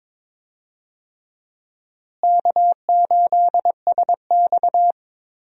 Morse code, heard as a keyed tone, about 22 wpm, 700 Hz. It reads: K8SX
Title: USA Call Signs - 4 Characters - Repeated 3x - 22wpm